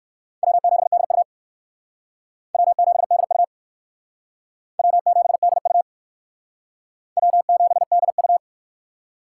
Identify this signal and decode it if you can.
Morse code, heard as a keyed tone, about 59 wpm, 700 Hz. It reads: W7DU W7DU W7DU W7DU